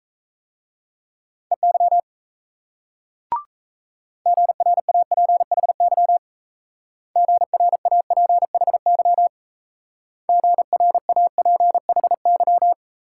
Morse code, heard as a keyed tone, about 35 wpm, 700 Hz. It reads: EY GRAPHY GRAPHY GRAPHY